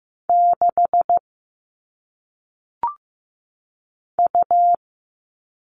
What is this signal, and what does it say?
Morse code, heard as a keyed tone, about 15 wpm, 700 Hz. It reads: 6 U
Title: All Cut Numbers - 1-Digit 15wpm